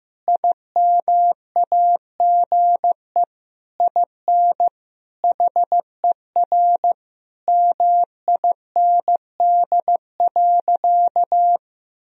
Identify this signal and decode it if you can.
Morse code, heard as a keyed tone, about 15 wpm, 700 Hz. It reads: IMAGE IN HER MIND.
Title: Sentences from Top 1000 Words - Review Part 4 - 15wpm